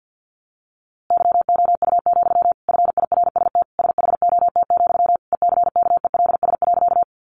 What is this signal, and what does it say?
Morse code, heard as a keyed tone, about 50 wpm, 700 Hz. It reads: YOU, FIRST SHOT, ECKELS.
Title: A Sound Of Thunder 50wpm